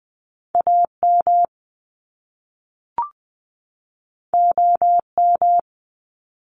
Morse code, heard as a keyed tone, about 20 wpm, 700 Hz. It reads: AM OM
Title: Top Two Letter Combinations in Words 20wpm